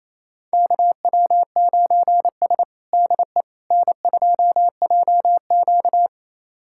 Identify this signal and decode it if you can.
Morse code, heard as a keyed tone, about 28 wpm, 700 Hz. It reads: KW9S DE N2JQ
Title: USA Callsign DE Callsign - 4 Characters - 28wpm